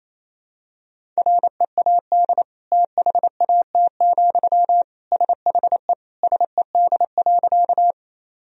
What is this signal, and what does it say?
Morse code, heard as a keyed tone, about 28 wpm, 700 Hz. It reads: READ THAT, SHE SED.
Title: Sentences from Top 200 Words and Abbreviations 28wpm